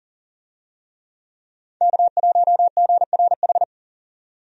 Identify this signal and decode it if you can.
Morse code, heard as a keyed tone, about 40 wpm, 700 Hz. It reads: K1GRH
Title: USA Call Signs - 5 Characters - 40wpm